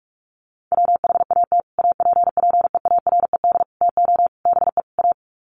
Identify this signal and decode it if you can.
Morse code, heard as a keyed tone, about 44 wpm, 700 Hz. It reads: WHAT APPEARED TO BE A
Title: Sets of 5 Words Encore2 45wpm